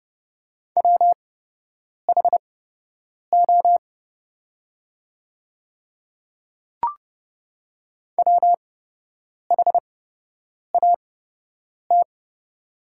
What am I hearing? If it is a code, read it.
Morse code, heard as a keyed tone, about 29 wpm, spaced as for 8 wpm, 700 Hz. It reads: WHO WHAT
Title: 8x Character Spacing - Top 100 Words - 30wpm